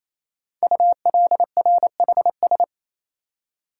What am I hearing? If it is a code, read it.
Morse code, heard as a keyed tone, about 27 wpm, 700 Hz. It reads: ULRHS
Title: Code Groups - 5 Letters - 28wpm